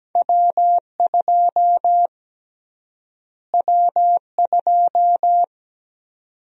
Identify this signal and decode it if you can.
Morse code, heard as a keyed tone, about 17 wpm, 700 Hz. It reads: W2 W2
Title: USA Call Sign Prefixes - Repeated 2x - 17wpm